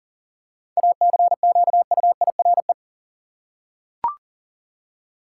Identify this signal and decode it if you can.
Morse code, heard as a keyed tone, about 40 wpm, 700 Hz. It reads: ACQUIRE
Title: Seven Letter Words - 40wpm